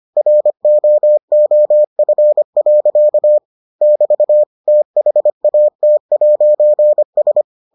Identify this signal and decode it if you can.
Morse code, heard as a keyed tone, about 25 wpm, 600 Hz. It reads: ROOF. <BT> THAT'S